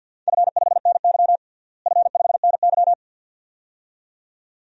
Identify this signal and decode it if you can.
Morse code, heard as a keyed tone, about 50 wpm, 700 Hz. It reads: F5NY F5NY
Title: DXCC-100 Call Signs - 4 Characters - Repeated 2x - 50wpm